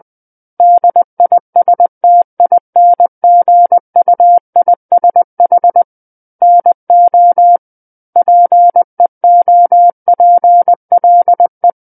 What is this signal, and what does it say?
Morse code, heard as a keyed tone, about 20 wpm, 700 Hz. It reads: DISTINGUISH NO PEOPLE
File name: 20 WPM